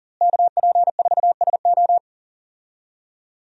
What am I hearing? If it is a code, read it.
Morse code, heard as a keyed tone, about 40 wpm, 700 Hz. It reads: KP4SO
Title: DXCC-100 Call Signs - 5 Characters - Repeated 2x - 40wpm